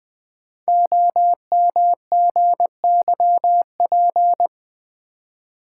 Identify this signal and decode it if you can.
Morse code, heard as a keyed tone, about 20 wpm, 700 Hz. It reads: OMGYP
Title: Code Groups - 5 Letters - 20wpm